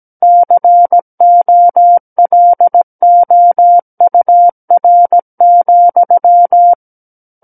Morse code, heard as a keyed tone, about 17 wpm, 700 Hz. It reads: COLOUR,
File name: Book One / 08-Chapter 8 - 17 WPM